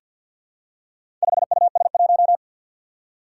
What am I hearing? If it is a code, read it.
Morse code, heard as a keyed tone, about 47 wpm, 700 Hz. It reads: 5US1
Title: Four Challenging Letter-Number Combinations 50wpm